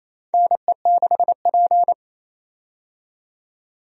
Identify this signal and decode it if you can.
Morse code, heard as a keyed tone, about 28 wpm, 700 Hz. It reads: NE6P
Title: USA Call Signs - 4 Characters - 28wpm